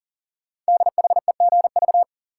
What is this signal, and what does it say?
Morse code, heard as a keyed tone, about 40 wpm, 700 Hz. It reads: DHEGV
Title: Code Groups - 5 Letters - 40wpm